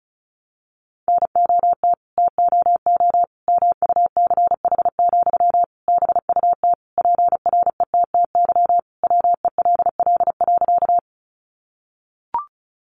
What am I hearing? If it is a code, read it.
Morse code, heard as a keyed tone, about 34 wpm, 700 Hz. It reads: NOT TOO MUCH, BUT PRETTY WELL.
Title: Sentences from Top 1000 Words 35wpm